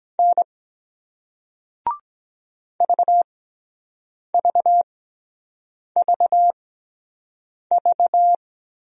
Morse code, heard as a keyed tone, about 24 wpm, spaced as for 7 wpm, 700 Hz. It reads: N VVVV